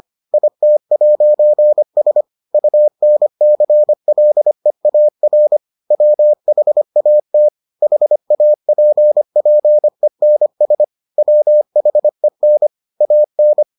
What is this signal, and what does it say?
Morse code, heard as a keyed tone, about 25 wpm, 600 Hz. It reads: IT'S UNCLEAR WHAT HAPPENS WHEN AN